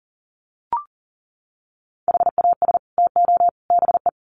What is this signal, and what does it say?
Morse code, heard as a keyed tone, about 40 wpm, 700 Hz. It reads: HAS TO BE
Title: Sets of 3 Words 40wpm